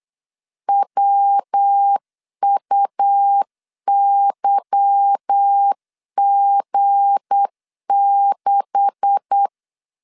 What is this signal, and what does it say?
Morse code, heard as a keyed tone, about 8 wpm, 800 Hz. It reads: WUYG6
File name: LEZIONI / 9 286 8wpm 3425